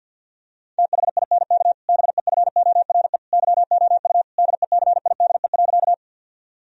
Morse code, heard as a keyed tone, about 52 wpm, 700 Hz. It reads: THINK BEFORE YOU DECIDE.
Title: Sentences from Top 1000 Words - Review Part 3 - 50wpm